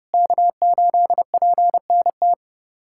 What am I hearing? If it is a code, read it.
Morse code, heard as a keyed tone, about 30 wpm, 700 Hz. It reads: K8PNT
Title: USA Callsign DE Callsign - 5 Characters - 30wpm